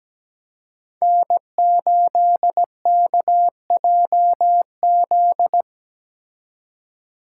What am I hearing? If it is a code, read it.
Morse code, heard as a keyed tone, about 17 wpm, 700 Hz. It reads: N8KJZ
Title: Code Groups - 5 Letter:Numbers - 17wpm